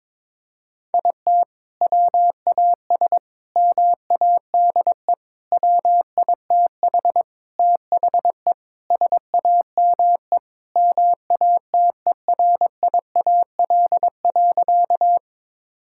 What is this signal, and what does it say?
Morse code, heard as a keyed tone, about 22 wpm, 700 Hz. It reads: IT WAS MADE WITH THE SAME MATERIAL.
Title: Sentences from Top 1000 Words - Review Part 5 - 22wpm